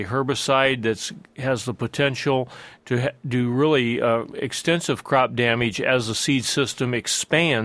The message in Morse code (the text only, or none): none